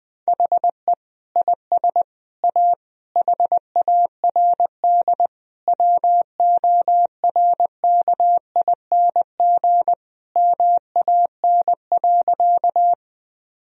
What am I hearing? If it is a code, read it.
Morse code, heard as a keyed tone, about 21 wpm, 700 Hz. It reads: HE IS A HARD WORKING MAN.
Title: Sentences from Top 1000 Words - Review Part 4 - 20wpm